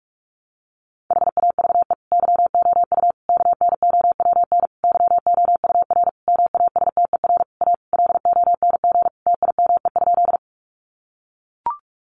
Morse code, heard as a keyed tone, about 44 wpm, 700 Hz. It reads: HAVE YOU KNOWN YOUR MASTER A LONG TIME?
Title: Sentences from Top 1300 Words 45wpm